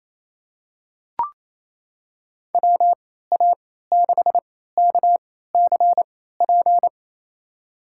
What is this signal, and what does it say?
Morse code, heard as a keyed tone, about 28 wpm, spaced as for 16 wpm, 700 Hz. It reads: WA6KCP
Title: USA Call Signs - 6 Characters - 3x Character Spacing - 28wpm